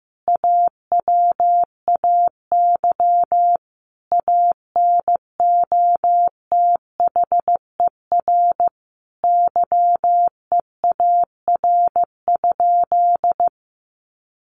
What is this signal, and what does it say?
Morse code, heard as a keyed tone, about 15 wpm, 700 Hz. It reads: AWAY ANOTHER YEAR?
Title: Sentences from Top 1500 Words 15wpm